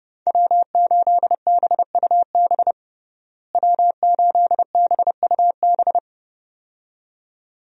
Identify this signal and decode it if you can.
Morse code, heard as a keyed tone, about 30 wpm, 700 Hz. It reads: W8BUB W8BUB
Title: USA Call Signs - 5 Characters - Repeated 2x - 30wpm